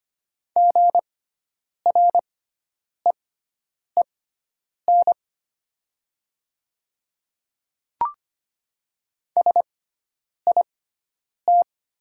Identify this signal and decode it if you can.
Morse code, heard as a keyed tone, about 25 wpm, spaced as for 9 wpm, 700 Hz. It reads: GREEN SIT
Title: 6x Character Spacing - Top 500 Words - 25wpm